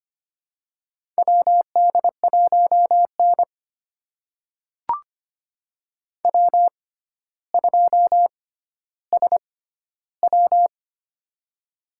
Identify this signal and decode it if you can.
Morse code, heard as a keyed tone, about 25 wpm, spaced as for 13 wpm, 700 Hz. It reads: WD1N W2SW
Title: USA Call Signs - 4 Characters - 6x Character Spacing - 25wpm